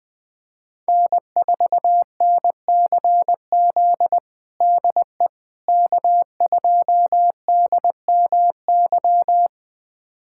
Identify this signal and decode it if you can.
Morse code, heard as a keyed tone, about 21 wpm, 700 Hz. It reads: N4NCZ DE K2DMY
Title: USA Callsign DE Callsign - 5 Characters - 20wpm